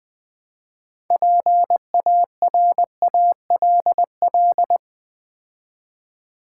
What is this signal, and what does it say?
Morse code, heard as a keyed tone, about 20 wpm, 700 Hz. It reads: PARALL
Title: Build A Word - 3 Letters At A Time - 8-Letter Words - 20wpm